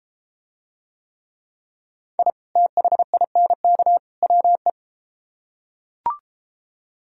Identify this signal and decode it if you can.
Morse code, heard as a keyed tone, about 30 wpm, 700 Hz. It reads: I THINK WE